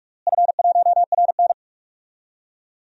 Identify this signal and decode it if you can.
Morse code, heard as a keyed tone, about 41 wpm, 700 Hz. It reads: F1RN
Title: DXCC-100 Call Signs - 4 Characters - Repeated 2x - 45wpm